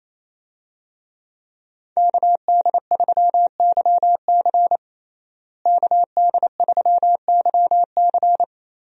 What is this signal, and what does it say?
Morse code, heard as a keyed tone, about 27 wpm, 700 Hz. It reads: KD3YC KD3YC
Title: USA Call Signs - 5 Characters - Repeated 3x - 28wpm